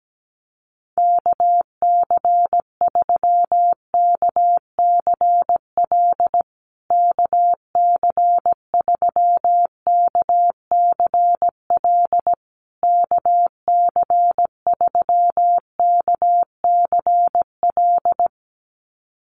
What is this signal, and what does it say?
Morse code, heard as a keyed tone, about 17 wpm, 700 Hz. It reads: KC3KCL KC3KCL KC3KCL
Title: USA Call Signs Repeated 3x 17wpm